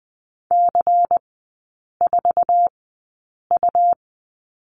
Morse code, heard as a keyed tone, about 20 wpm, spaced as for 8 wpm, 690 Hz. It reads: C4U
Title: 133 Lesson 22e - Question Mark - Three Characters 20wpm